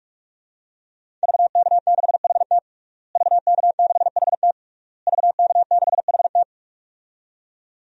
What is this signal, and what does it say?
Morse code, heard as a keyed tone, about 46 wpm, 700 Hz. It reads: VK6HT VK6HT VK6HT
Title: DXCC-100 Call Signs - 5 Characters - Repeated 3x - 45wpm